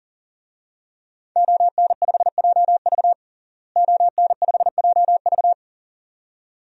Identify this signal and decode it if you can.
Morse code, heard as a keyed tone, about 40 wpm, 700 Hz. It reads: ON5JV ON5JV
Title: DXCC-100 Call Signs - 5 Characters - Repeated 2x - 40wpm